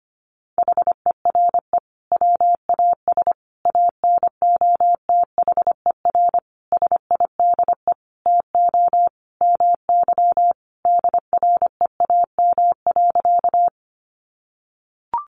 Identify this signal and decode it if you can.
Morse code, heard as a keyed tone, about 25 wpm, 700 Hz. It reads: HERE WAS ANOTHER SIDE TO MY DREAM.
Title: Sentences from Top 1500 Words 25wpm